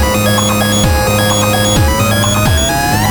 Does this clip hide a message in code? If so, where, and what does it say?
No.